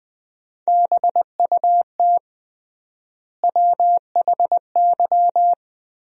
Morse code, heard as a keyed tone, about 20 wpm, 700 Hz. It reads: BUT WHY
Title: 3x Word-Spacing - Sentences from Top 300 Words - 20wpm